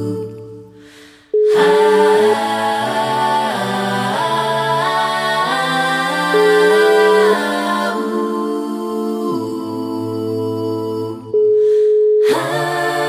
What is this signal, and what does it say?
Ringback tone (Europe, 425 Hz), with noise.